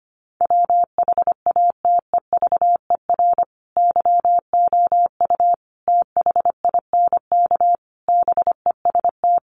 Morse code, heard as a keyed tone, about 25 wpm, 700 Hz. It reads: WHATEVER YOU THINK BEST
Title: Sentences from Top 1100 Words 25wpm